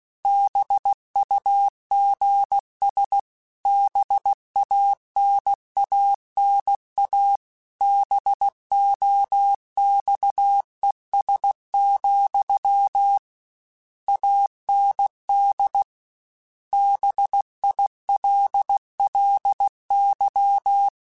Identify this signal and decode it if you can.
Morse code, heard as a keyed tone, about 16 wpm, 800 Hz. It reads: BUGS BANANA BOXES, AND BILLY
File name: Misc / Misc-mp3s / BIG @15